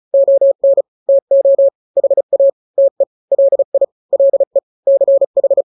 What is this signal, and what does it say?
Morse code, heard as a keyed tone, about 35 wpm, 550 Hz. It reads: ON TO HA TE LI LE CH
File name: n-grams / 2-grams 35 30 WPM